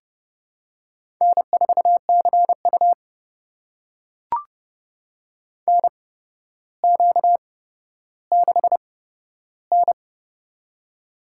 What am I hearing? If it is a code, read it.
Morse code, heard as a keyed tone, about 31 wpm, spaced as for 13 wpm, 700 Hz. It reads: N4CU NQ6N
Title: USA Call Signs - 4 Characters - 8x Character Spacing - 30wpm